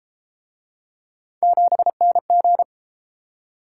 Morse code, heard as a keyed tone, about 32 wpm, 700 Hz. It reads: 7NG